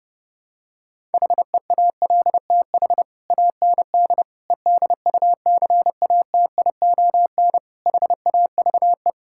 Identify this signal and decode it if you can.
Morse code, heard as a keyed tone, about 30 wpm, 700 Hz. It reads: HEALTH AND EDUCATION HAVE
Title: Sentences from Top 1000 Words - Review Part 6 - 30wpm